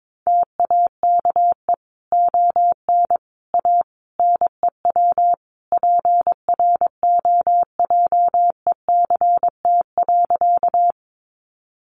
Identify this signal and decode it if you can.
Morse code, heard as a keyed tone, about 22 wpm, 700 Hz. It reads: TAKE ON A NEW PROJECT.
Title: Sentences from Top 1400 Words 22wpm